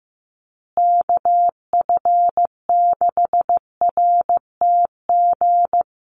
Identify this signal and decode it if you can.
Morse code, heard as a keyed tone, about 15 wpm, 700 Hz. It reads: KF6RTG
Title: USA Call Signs Repeated 3x 15wpm